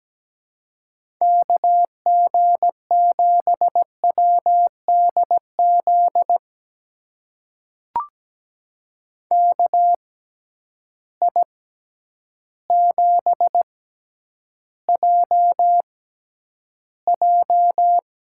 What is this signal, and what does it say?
Morse code, heard as a keyed tone, about 17 wpm, spaced as for 9 wpm, 700 Hz. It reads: KG7WDZ KI7JJ
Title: USA Call Signs - 6 Characters - 6x Character Spacing - 17wpm